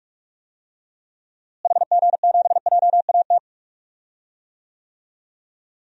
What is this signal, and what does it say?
Morse code, heard as a keyed tone, about 46 wpm, 700 Hz. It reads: HG7JAT
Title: DXCC-100 Call Signs - 6 Characters - Repeated 2x - 45wpm